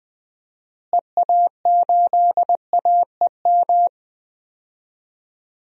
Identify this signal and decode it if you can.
Morse code, heard as a keyed tone, about 20 wpm, 700 Hz. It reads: EA8AEM
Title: DXCC-100 Call Signs - 6 Characters - Repeated 3x - 20wpm